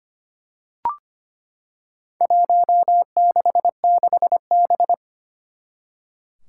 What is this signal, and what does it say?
Morse code, heard as a keyed tone, about 25 wpm, 700 Hz. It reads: 166B
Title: Four Challenging Letter-Number Combinations 25wpm